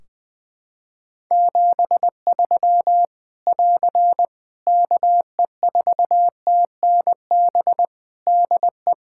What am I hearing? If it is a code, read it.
Morse code, heard as a keyed tone, about 20 wpm, 700 Hz. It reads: 73 <AR> KE4TNB DE
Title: K1USN SST QSO Practice 20wpm